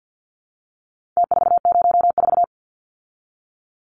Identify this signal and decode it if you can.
Morse code, heard as a keyed tone, about 50 wpm, 700 Hz. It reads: T404
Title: Numbers - Contest Style - 4-Digit 50wpm